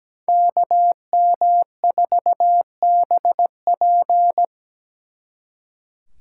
Morse code, heard as a keyed tone, about 20 wpm, 700 Hz. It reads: KM4BP
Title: Speed Racing - USA 2x2 Call Signs 17wpm